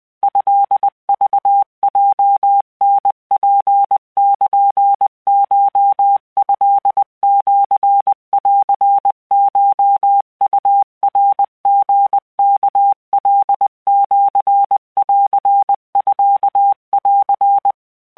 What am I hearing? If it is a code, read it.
Morse code (Japanese Wabun code), heard as a keyed tone, about 20 wpm, 800 Hz. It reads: トクヲタツルコトシンコウナリワカシンミン
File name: keuikuchiyokugo20wptan